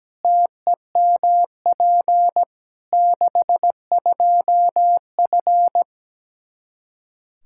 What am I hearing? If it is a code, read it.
Morse code, heard as a keyed tone, about 17 wpm, 700 Hz. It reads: TEMP 62F
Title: K1USN SST QSO Practice 17wpm